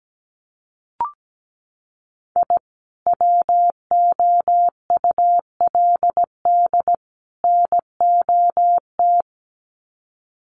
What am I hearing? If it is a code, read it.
Morse code, heard as a keyed tone, about 17 wpm, 700 Hz. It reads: I WOULD NOT
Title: Sets of 3 Words 17wpm